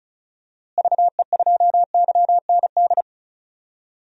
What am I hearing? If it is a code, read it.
Morse code, heard as a keyed tone, about 36 wpm, 700 Hz. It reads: VE2YND
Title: DXCC-100 Call Signs - 6 Characters - Repeated 3x - 35wpm